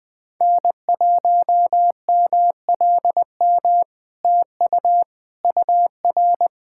Morse code, heard as a keyed tone, about 20 wpm, 700 Hz. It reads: N1MLM TU UR
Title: SOTA QSO Practice 20wpm